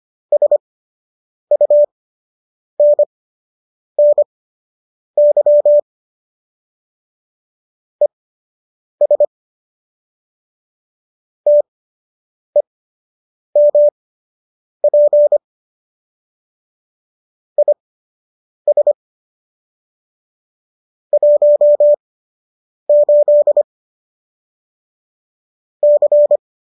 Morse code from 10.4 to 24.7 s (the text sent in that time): TEMP IS 18